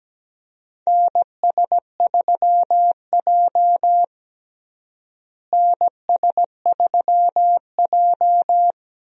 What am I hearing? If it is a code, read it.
Morse code, heard as a keyed tone, about 17 wpm, 700 Hz. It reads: NS3J NS3J
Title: USA Call Signs - 4 Characters - Repeated 3x - 17wpm